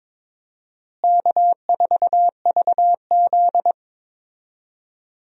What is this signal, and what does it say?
Morse code, heard as a keyed tone, about 22 wpm, 700 Hz. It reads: K4VZ
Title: USA Call Signs - 4 Characters - Repeated 3x - 22wpm